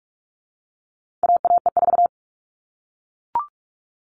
Morse code, heard as a keyed tone, about 44 wpm, 700 Hz. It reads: AAE4
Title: All Cut Numbers - 4-Digit 45wpm